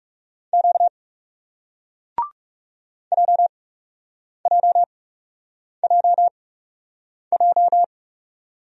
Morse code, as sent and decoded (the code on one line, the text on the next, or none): --.- / .--- .--- .--- .---
Q JJJJ